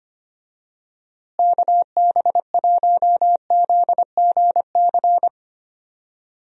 Morse code, heard as a keyed tone, about 25 wpm, 700 Hz. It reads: KB1ZGC
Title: USA Call Signs - 6 Characters - Repeated 2x - 25wpm